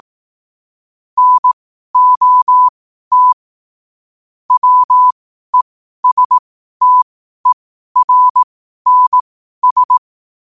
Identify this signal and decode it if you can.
Morse code, heard as a keyed tone, about 18 wpm, spaced as for 12 wpm, 1000 Hz. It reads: NOT WESTERNS
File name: qotdfiles / 12wpm